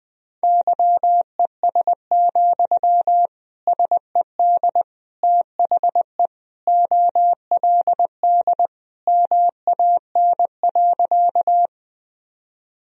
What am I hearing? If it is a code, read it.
Morse code, heard as a keyed tone, about 20 wpm, 700 Hz. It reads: YES, SED THE OLD MAN.
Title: Sentences from Top 500 Words and Abbreviations 20wpm